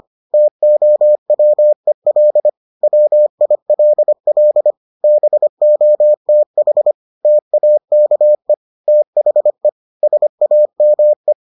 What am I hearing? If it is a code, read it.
Morse code, heard as a keyed tone, about 25 wpm, 600 Hz. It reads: TOWEL WILL BOTH TAKE THE SAME